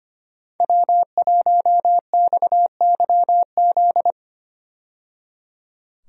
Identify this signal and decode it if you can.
Morse code, heard as a keyed tone, about 25 wpm, 700 Hz. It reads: W1XYZ
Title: USA Call Signs - 5 Characters - 25wpm